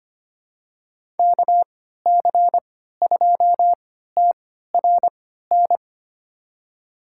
Morse code, heard as a keyed tone, about 25 wpm, spaced as for 14 wpm, 700 Hz. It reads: KC2TRN
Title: USA Call Signs - 6 Characters - 3x Character Spacing - 25wpm